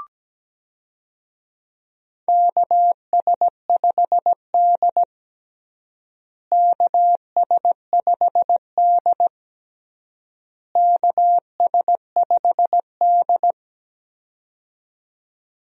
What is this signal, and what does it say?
Morse code, heard as a keyed tone, about 17 wpm, 700 Hz. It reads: KS5D KS5D KS5D
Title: USA Call Signs - 4 Characters - Repeated 3x - 17wpm